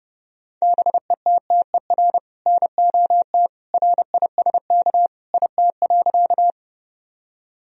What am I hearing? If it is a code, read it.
Morse code, heard as a keyed tone, about 32 wpm, 700 Hz. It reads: BETTER NOT RISK IT.
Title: Sentences from Top 1000 Words - Review Part 4 - 30wpm